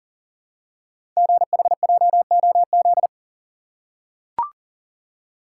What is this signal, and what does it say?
Morse code, heard as a keyed tone, about 43 wpm, 700 Hz. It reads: GHJOZ
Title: Code Groups - 5 Letters - 40wpm